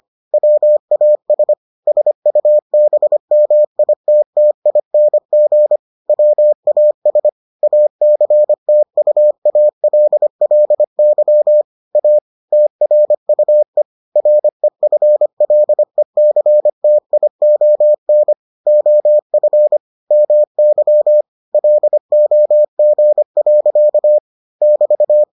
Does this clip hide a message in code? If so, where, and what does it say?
Yes: Morse code from 0.3 to 25.2 s, heard as a keyed tone, about 25 wpm, 600 Hz. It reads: WAS SUBMITTING WAS ACTUALLY A TRUE REFLECTION OF MY LOG. <BT>